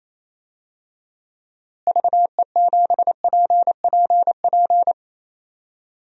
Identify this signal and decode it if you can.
Morse code, heard as a keyed tone, about 27 wpm, 700 Hz. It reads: VE7PPP
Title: DXCC-100 Call Signs - 6 Characters - Repeated 3x - 28wpm